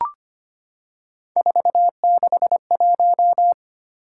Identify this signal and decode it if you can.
Morse code, heard as a keyed tone, about 25 wpm, 700 Hz. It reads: 461
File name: Three Letter-Number Combinations - 25wpm